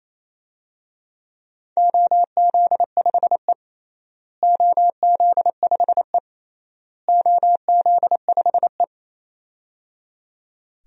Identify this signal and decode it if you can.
Morse code, heard as a keyed tone, about 27 wpm, 700 Hz. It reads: OZ5E OZ5E OZ5E
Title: DXCC-100 Call Signs - 4 Characters - Repeated 3x - 28wpm